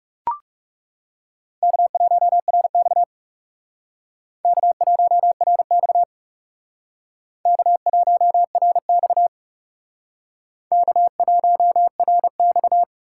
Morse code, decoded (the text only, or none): K1RX K1RX K1RX K1RX